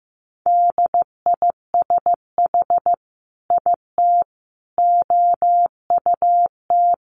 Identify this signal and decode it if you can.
Morse code, heard as a keyed tone, about 15 wpm, 700 Hz. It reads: DISH IT OUT
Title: Sentences from Top 1300 Words 15wpm